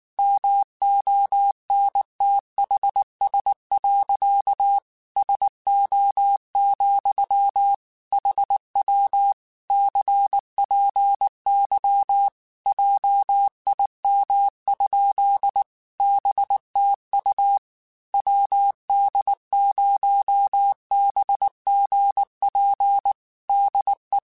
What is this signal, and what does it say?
Morse code, heard as a keyed tone, about 19 wpm, 800 Hz. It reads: MONTHS. SO, HW CPY JIM? BTU WD0BGP DE
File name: QSOs / QSOs-mp3s / WD0BGP 1@18